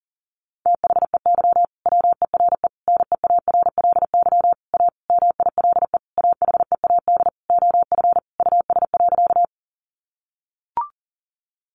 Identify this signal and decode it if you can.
Morse code, heard as a keyed tone, about 40 wpm, 700 Hz. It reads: THEY WERE NEARLY A MILE AHEAD OF US.
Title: Sentences from Top 1300 Words 40wpm